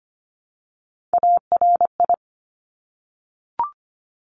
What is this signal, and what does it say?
Morse code, heard as a keyed tone, about 25 wpm, 700 Hz. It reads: ARI
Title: Top Three Letter Combinations in Words 25wpm